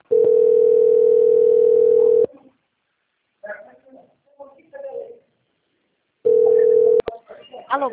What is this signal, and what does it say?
Ringback tone (North America), 440 + 480 Hz, with speech.
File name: out- - - - - .48039